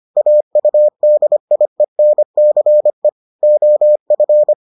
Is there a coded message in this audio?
Yes: Morse code from 0.2 to 4.5 s, heard as a keyed tone, about 25 wpm, 600 Hz. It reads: AUDIENCE OF